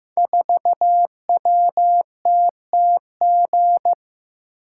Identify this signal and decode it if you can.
Morse code, heard as a keyed tone, about 15 wpm, 700 Hz. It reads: 4WTTG
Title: Code Groups - 5 Letter:Numbers - 15wpm